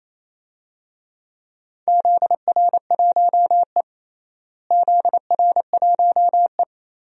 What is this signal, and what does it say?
Morse code, heard as a keyed tone, about 28 wpm, 700 Hz. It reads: ZR1E ZR1E
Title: DXCC-100 Call Signs - 4 Characters - Repeated 2x - 28wpm